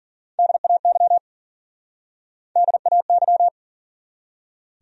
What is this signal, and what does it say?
Morse code, heard as a keyed tone, about 52 wpm, 700 Hz. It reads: DAY DAY